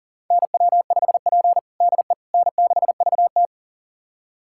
Morse code, heard as a keyed tone, about 40 wpm, 690 Hz. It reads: NW5P DE N6VT
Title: USA Callsign DE Callsign - 4 Characters - 40wpm